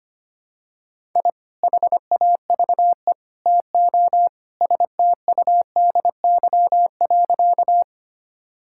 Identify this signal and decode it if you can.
Morse code, heard as a keyed tone, about 25 wpm, 700 Hz. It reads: I HAVE TO STUDY.
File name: practice / 25wpm / Sentences with 3-Word Phrases - 25wpm